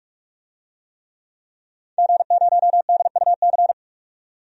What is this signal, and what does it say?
Morse code, heard as a keyed tone, about 42 wpm, 700 Hz. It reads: G0DUC
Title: DXCC-100 Call Signs - 5 Characters - 45wpm